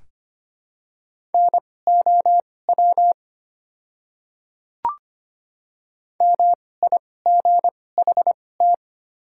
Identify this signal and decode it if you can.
Morse code, heard as a keyed tone, about 25 wpm, spaced as for 18 wpm, 690 Hz. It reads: NOW MIGHT
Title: 2x Character Spacing - Top 300 Words - 25wpm